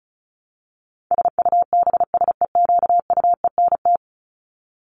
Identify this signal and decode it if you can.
Morse code, heard as a keyed tone, about 34 wpm, 700 Hz. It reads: SUBSEQUENT
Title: LY Suffix - Top 100 Words with Base Words 35wpm